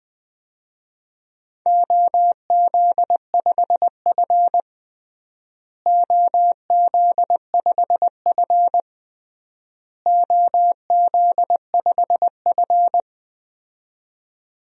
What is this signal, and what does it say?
Morse code, heard as a keyed tone, about 20 wpm, 700 Hz. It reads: OZ5F OZ5F OZ5F
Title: DXCC-100 Call Signs - 4 Characters - Repeated 3x - 20wpm